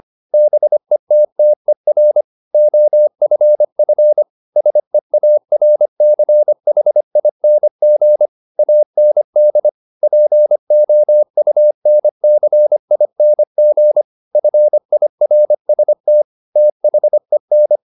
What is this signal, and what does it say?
Morse code, heard as a keyed tone, about 25 wpm, 600 Hz. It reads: BETTER OFF SEARCHING AND POUNCING FIRST THEN